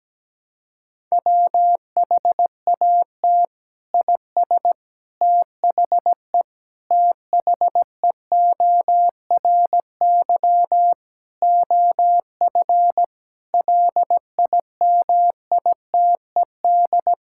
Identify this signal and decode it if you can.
Morse code, heard as a keyed tone, about 17 wpm, 700 Hz. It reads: WHAT IS THE THEORY OF LIMITED
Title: Sentences from Top 1000 Words - Review Part 6 - 17wpm